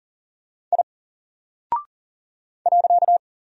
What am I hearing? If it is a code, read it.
Morse code, heard as a keyed tone, about 40 wpm, 700 Hz. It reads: I .
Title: Single Character (Letter:Number:Punctuation:Prosign) Alt 40wpm